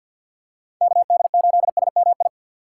Morse code, heard as a keyed tone, about 51 wpm, 700 Hz. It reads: KD8SMI
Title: USA Call Signs - 3x Character Spacing - 50wpm